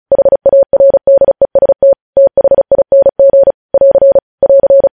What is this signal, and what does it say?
Morse code, heard as a keyed tone, about 36 wpm, 550 Hz. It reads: HARDEST THING <AR> <AR>
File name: podcast / News headlines-1x-35x35